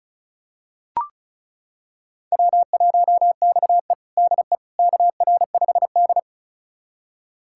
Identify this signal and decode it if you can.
Morse code, heard as a keyed tone, about 34 wpm, 700 Hz. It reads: W1XE DE KR5D
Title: USA Callsign DE Callsign - 4 Characters - 35wpm